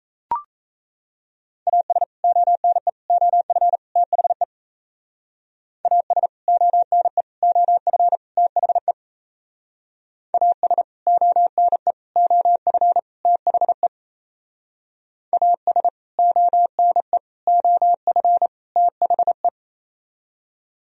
Morse code, heard as a keyed tone, about 35 wpm, 700 Hz. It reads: AS ONE OF THE AS ONE OF THE AS ONE OF THE AS ONE OF THE
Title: Speed-Racing - Sets of 4 Words 28wpm